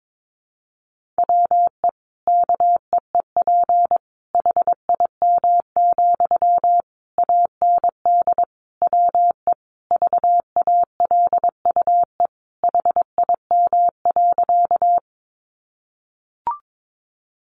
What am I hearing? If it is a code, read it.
Morse code, heard as a keyed tone, about 22 wpm, 700 Hz. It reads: WE KEEP HIM, AND WE VALUE HIM.
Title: Sentences from Top 600 Words 22wpm